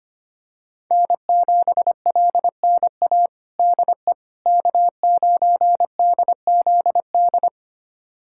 Morse code, heard as a keyed tone, about 25 wpm, 700 Hz. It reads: N7LNA DE K9DZD
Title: USA Callsign DE Callsign - 5 Characters - 25wpm